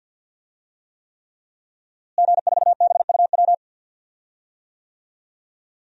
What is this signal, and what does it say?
Morse code, heard as a keyed tone, about 53 wpm, 700 Hz. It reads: G4BUW
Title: DXCC-100 Call Signs - 5 Characters - Repeated 3x - 50wpm